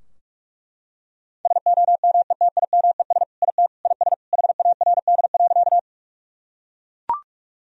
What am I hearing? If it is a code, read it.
Morse code, heard as a keyed tone, about 42 wpm, 700 Hz. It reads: SOMETIMES IT IS HARD.
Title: Sentences from Top 1000 Words - Review Part 3 - 45wpm